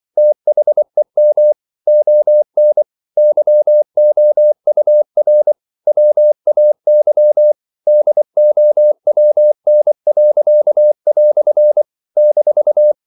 Morse code, heard as a keyed tone, about 24 wpm, 600 Hz. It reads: THEM ON YOUR WAY DOWN." -
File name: CW / Practice / EnglishQuotations0224